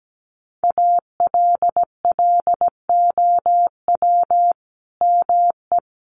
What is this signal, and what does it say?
Morse code, heard as a keyed tone, about 17 wpm, 700 Hz. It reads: ALLOW ME
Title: Sentences from Top 1100 Words 17wpm